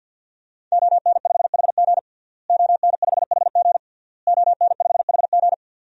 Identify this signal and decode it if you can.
Morse code, heard as a keyed tone, about 52 wpm, 700 Hz. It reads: ON5HG ON5HG ON5HG
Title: DXCC-100 Call Signs - 5 Characters - Repeated 3x - 50wpm